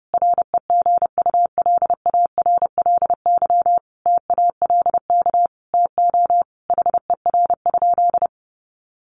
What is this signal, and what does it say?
Morse code, heard as a keyed tone, about 30 wpm, 700 Hz. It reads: REGULARLY TALK TO HER?
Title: Sentences from Top 1400 Words 30wpm